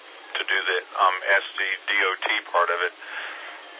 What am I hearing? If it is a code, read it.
Emergency Alert System (SAME) tones, heard as harsh buzzing bursts, with speech. It is heard from 0.4 to 3.3 s.